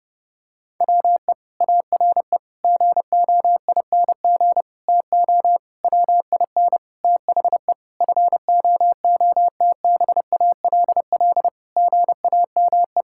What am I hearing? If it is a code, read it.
Morse code, heard as a keyed tone, about 30 wpm, 700 Hz. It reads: WE ARE GOING TO WIN THE FOOTBALL GAME
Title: Sentences from Top 1600 Words 30wpm